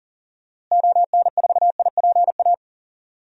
Morse code, heard as a keyed tone, about 40 wpm, 700 Hz. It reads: ON4IPA
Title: DXCC-100 Call Signs - 6 Characters - Repeated 3x - 40wpm